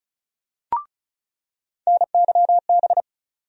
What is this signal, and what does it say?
Morse code, heard as a keyed tone, about 36 wpm, 700 Hz. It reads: NYB
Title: USA Call Sign Postfixes - 3 Letters 35wpm